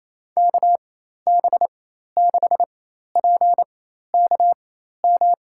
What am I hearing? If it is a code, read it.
Morse code, heard as a keyed tone, about 28 wpm, spaced as for 13 wpm, 700 Hz. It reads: KB6PKM
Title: USA Call Signs - 6 Characters - 4x Character Spacing - 28wpm